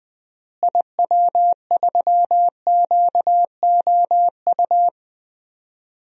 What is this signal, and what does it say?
Morse code, heard as a keyed tone, about 20 wpm, 700 Hz. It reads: IW3QOU
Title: DXCC-100 Call Signs - 6 Characters - Repeated 3x - 20wpm